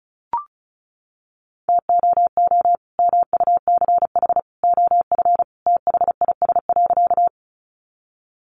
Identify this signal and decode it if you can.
Morse code, heard as a keyed tone, about 34 wpm, 700 Hz. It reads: TOO MUCH OF THIS.
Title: Sentences from Top 200 Words 35wpm